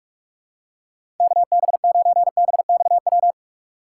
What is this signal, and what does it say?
Morse code, heard as a keyed tone, about 45 wpm, 700 Hz. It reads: KB9BXW
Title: USA Call Signs - 6 Characters - 4x Character Spacing - 45wpm